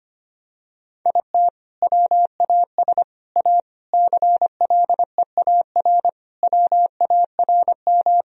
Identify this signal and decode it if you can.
Morse code, heard as a keyed tone, about 25 wpm, 700 Hz. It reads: IT WAS A CLEAR WARM
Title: Sentences from Top 1000 Words - Review Part 2 - 25wpm